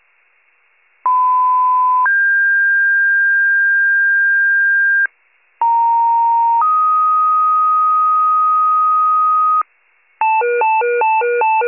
Two-tone page, heard as a one-second tone, then a three-second tone, about 1010 then 1600 Hz, about 930 then 1250 Hz, with noise. It is heard from 1.0 to 9.6 s.
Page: 82 FIRE